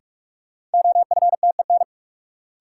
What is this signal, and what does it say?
Morse code, heard as a keyed tone, about 41 wpm, 700 Hz. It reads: OFTEN
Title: Sentences from Top 300 Words - New Word Drill - 45wpm